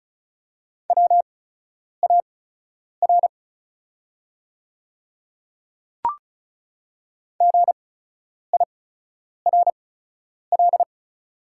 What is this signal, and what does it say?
Morse code, heard as a keyed tone, about 36 wpm, spaced as for 10 wpm, 690 Hz. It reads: WAR GIRL
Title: 8x Character Spacing - Top 500 Words - 35wpm